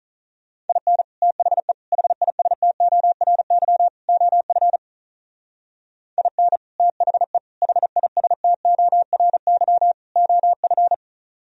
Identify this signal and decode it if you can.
Morse code, heard as a keyed tone, about 44 wpm, 700 Hz. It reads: IN THE HISTORY OF IN THE HISTORY OF
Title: Speed-Racing - Sets of 4 Words 35wpm